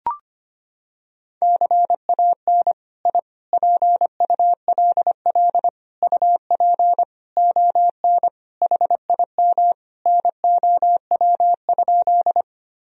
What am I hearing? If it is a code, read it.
Morse code, heard as a keyed tone, about 25 wpm, 700 Hz. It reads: CAN I PULL UP ON HIM NOW?
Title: Sentences from Top 1000 Words - Review Part 2 - 25wpm